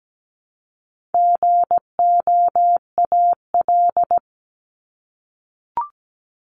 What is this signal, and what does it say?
Morse code, heard as a keyed tone, about 17 wpm, 700 Hz. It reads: GOAL
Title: Top 751-1000 Words 17wpm